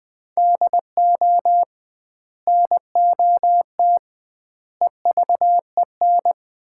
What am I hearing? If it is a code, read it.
Morse code, heard as a keyed tone, about 19 wpm, 700 Hz. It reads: DO NOT EVEN
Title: 2x Word-Spacing - Sets of 4 Words - 20wpm